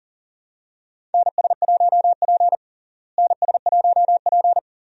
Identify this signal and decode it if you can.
Morse code, heard as a keyed tone, about 40 wpm, 700 Hz. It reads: NS1P NS1P
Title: USA Call Signs - 4 Characters - Repeated 3x - 40wpm